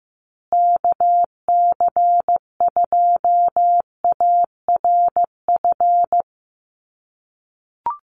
Morse code, heard as a keyed tone, about 15 wpm, 700 Hz. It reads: KC2ARF
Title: USA Call Signs Repeated 2x 15wpm